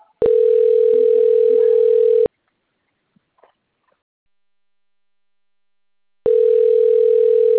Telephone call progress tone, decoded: ringback tone

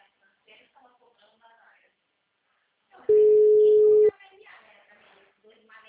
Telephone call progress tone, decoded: ringback tone

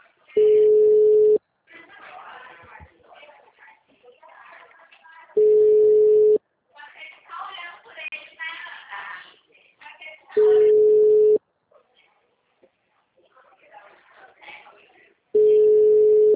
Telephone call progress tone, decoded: ringback tone

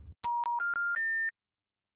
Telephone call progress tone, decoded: special information tone

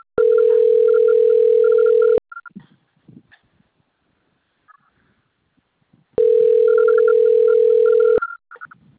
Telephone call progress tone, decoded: ringback tone

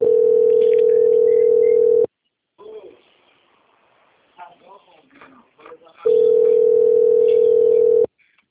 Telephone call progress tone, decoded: ringback tone

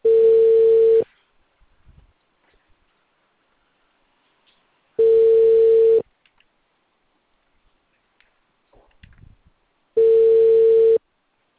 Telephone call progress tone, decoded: ringback tone

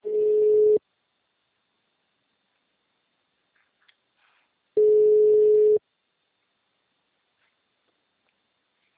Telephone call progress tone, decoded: ringback tone

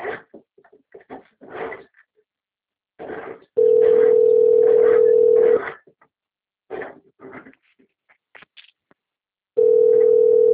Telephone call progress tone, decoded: ringback tone

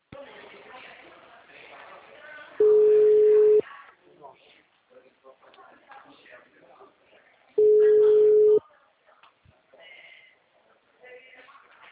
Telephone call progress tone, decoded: ringback tone

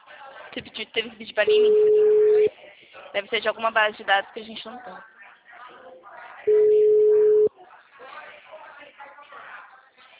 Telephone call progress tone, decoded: ringback tone

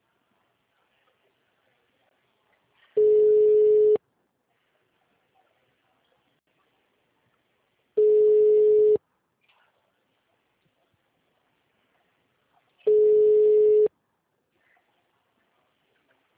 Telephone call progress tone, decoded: ringback tone

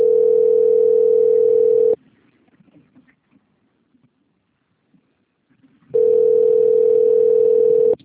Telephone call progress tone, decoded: ringback tone

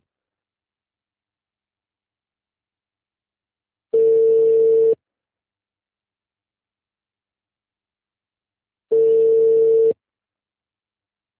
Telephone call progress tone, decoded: ringback tone

